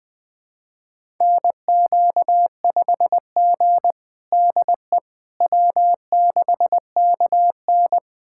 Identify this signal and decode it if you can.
Morse code, heard as a keyed tone, about 20 wpm, 700 Hz. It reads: NQ5G DE W6KN
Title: USA Callsign DE Callsign - 4 Characters - 20wpm